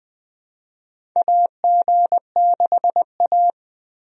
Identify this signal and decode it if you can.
Morse code, heard as a keyed tone, about 20 wpm, 700 Hz. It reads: AG6A
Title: USA Call Signs - 4 Characters - 20wpm